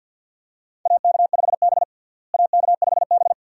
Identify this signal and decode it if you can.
Morse code, heard as a keyed tone, about 50 wpm, 700 Hz. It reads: AK5B AK5B
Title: USA Call Signs - 4 Characters - Repeated 2x - 50wpm